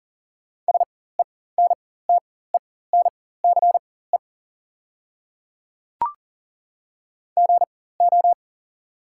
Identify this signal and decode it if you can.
Morse code, heard as a keyed tone, about 40 wpm, spaced as for 19 wpm, 690 Hz. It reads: SENTENCE GO
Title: 4x Character Spacing - Top 200 Words - 40wpm